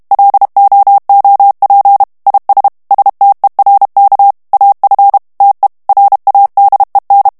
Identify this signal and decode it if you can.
Morse code, heard as a keyed tone, about 32 wpm, 800 Hz. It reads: LOOP IS STERK AF TE RADEN
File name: HOLLANDAIS texte clair 1 vitesse 30 mots minute